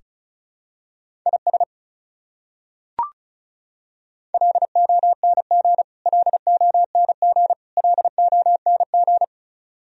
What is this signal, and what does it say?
Morse code, heard as a keyed tone, about 36 wpm, 700 Hz. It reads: IS LONG LONG LONG
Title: Top 100 Words Repeated 35wpm